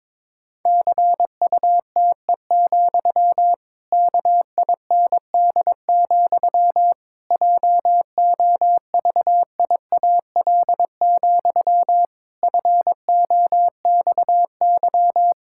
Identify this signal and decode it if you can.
Morse code, heard as a keyed tone, about 22 wpm, 700 Hz. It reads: CUTE, KIND, JOVIAL, FOXY
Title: Pangram Challenge v2 - 22wpm